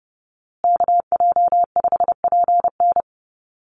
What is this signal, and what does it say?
Morse code, heard as a keyed tone, about 30 wpm, 700 Hz. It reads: KJ5PN
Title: USA Call Signs Repeated 2x 30wpm